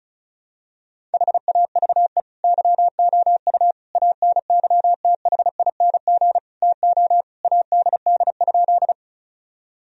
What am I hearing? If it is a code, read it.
Morse code, heard as a keyed tone, about 36 wpm, 700 Hz. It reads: HAVE YOU ANYTHING TO ADD?
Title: Sentences from Top 1000 Words - Review Part 1 - 35wpm